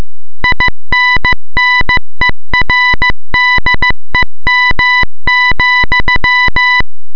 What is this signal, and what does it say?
Morse code, heard as a keyed tone, about 15 wpm, 2000 Hz. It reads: INNERDEM,